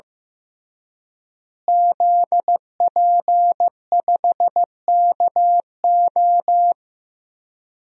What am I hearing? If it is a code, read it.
Morse code, heard as a keyed tone, about 15 wpm, 700 Hz. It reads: ZP5KO
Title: DXCC-100 Call Signs - 5 Characters - Repeated 3x - 15wpm